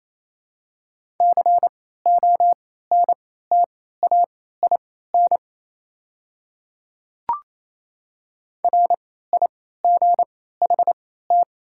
Morse code, heard as a keyed tone, about 27 wpm, spaced as for 16 wpm, 690 Hz. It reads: CONTAIN RIGHT
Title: 3x Character Spacing - Top 500 Words - 28wpm